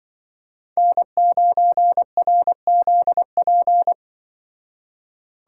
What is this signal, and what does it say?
Morse code, heard as a keyed tone, about 24 wpm, 700 Hz. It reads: N9RZP
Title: Speed Racing - USA 1x3 Call Signs 20wpm